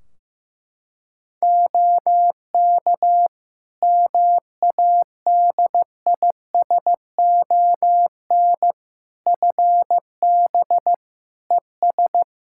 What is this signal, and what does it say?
Morse code, heard as a keyed tone, about 16 wpm, 700 Hz. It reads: OK MADISON FB ES
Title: Practice QSO - Exchange 2 - Info - 15wpm